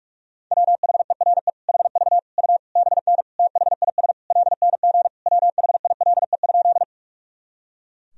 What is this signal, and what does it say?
Morse code, heard as a keyed tone, about 46 wpm, 700 Hz. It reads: WHERE HV U BN THIS LNG WHILE?
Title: Sentences from Top 300 Words and Abbreviations 45wpm